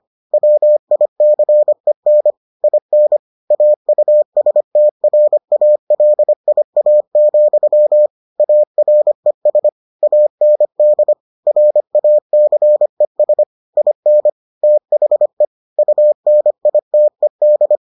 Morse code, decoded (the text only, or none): WICEN IN AUSTRALIA, ARES AND RACES IN THE UNITED